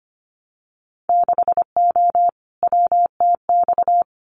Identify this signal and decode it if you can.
Morse code, heard as a keyed tone, about 25 wpm, 700 Hz. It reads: 6O WTX
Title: Winter Field Day Exchanges Encore 25wpm